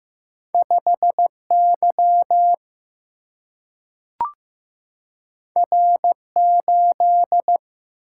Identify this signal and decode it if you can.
Morse code, heard as a keyed tone, about 15 wpm, 700 Hz. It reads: 5 Y R 8
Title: Single-Letter Single-Number 15wpm